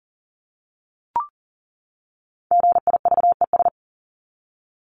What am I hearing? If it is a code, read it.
Morse code, heard as a keyed tone, about 40 wpm, 700 Hz. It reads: GIVES
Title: ES:S Suffix - Top 100 Words with Base Words 40wpm